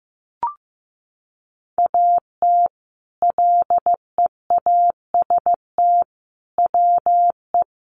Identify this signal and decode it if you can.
Morse code, heard as a keyed tone, about 15 wpm, 700 Hz. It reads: AT LEAST WE
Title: Sentences from Top 1300 Words 15wpm